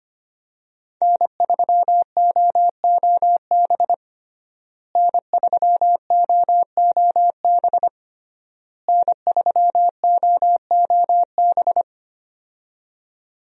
Morse code, heard as a keyed tone, about 25 wpm, 700 Hz. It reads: N3OOB N3OOB N3OOB
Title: USA Call Signs - 5 Characters - Repeated 3x - 25wpm